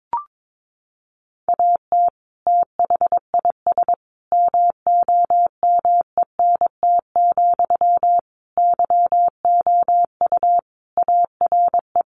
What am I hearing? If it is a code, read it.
Morse code, heard as a keyed tone, about 22 wpm, 700 Hz. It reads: AT THIS MOMENT, YOU ARE
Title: Sentences from Top 1000 Words 22wpm